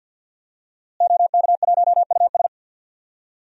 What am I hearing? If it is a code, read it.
Morse code, heard as a keyed tone, about 51 wpm, 700 Hz. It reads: OK1US
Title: DXCC-100 Call Signs - 5 Characters - Repeated 3x - 50wpm